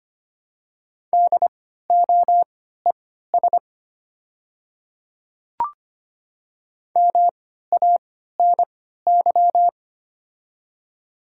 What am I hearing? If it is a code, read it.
Morse code, heard as a keyed tone, about 25 wpm, spaced as for 14 wpm, 700 Hz. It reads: DOES MANY
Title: 3x Character Spacing - Top 200 Words - 25wpm